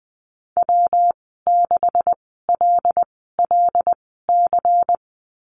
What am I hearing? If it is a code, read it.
Morse code, heard as a keyed tone, about 20 wpm, spaced as for 14 wpm, 700 Hz. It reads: W6LLC
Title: 154 Lesson 25g - Letter p - Call Signs 20wpm